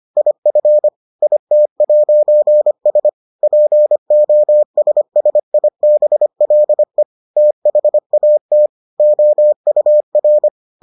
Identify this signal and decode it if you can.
Morse code, heard as a keyed tone, about 25 wpm, 600 Hz. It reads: IF IT'S POSSIBLE THAT OUR